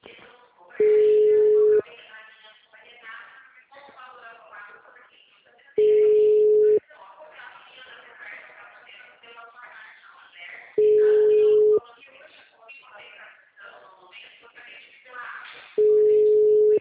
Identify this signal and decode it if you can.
Ringback tone (Europe, 425 Hz), with speech.
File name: out- -2009- - - .83310